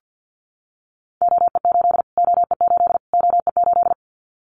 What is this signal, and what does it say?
Morse code, heard as a keyed tone, about 49 wpm, 700 Hz. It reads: OE8 OE8 OE8
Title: Callsign Prefixes for Top 25 DXCC - Repeated 3x - 50wpm